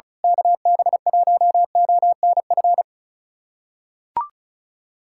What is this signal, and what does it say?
Morse code, heard as a keyed tone, about 36 wpm, 700 Hz. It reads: KB1ONF
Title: USA Callsign DE Callsign - 6 Characters - 35wpm